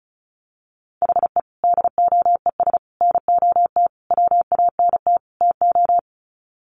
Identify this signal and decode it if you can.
Morse code, heard as a keyed tone, about 34 wpm, 700 Hz. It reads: HE DOES NOT WANT TO
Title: Sets of 5 Words 35wpm